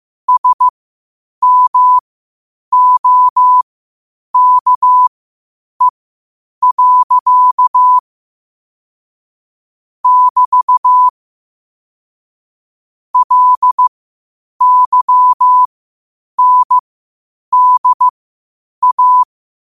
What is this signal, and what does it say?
Morse code, heard as a keyed tone, about 15 wpm, spaced as for 8 wpm, 1000 Hz. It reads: SMOKE. <BT> LYNDA
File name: qotdfiles / 8wpm